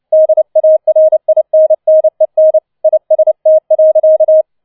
Morse code, heard as a keyed tone, about 29 wpm, 630 Hz. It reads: DARINNEN IST.
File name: 006 Sechster Streich 30wpm